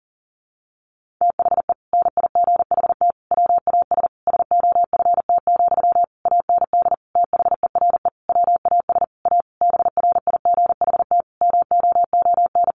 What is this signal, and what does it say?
Morse code, heard as a keyed tone, about 40 wpm, 700 Hz. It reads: THE NIGHT WAS SOFT, AND THERE WAS A BRIGHT MOON
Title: Sentences from Top 1500 Words 40wpm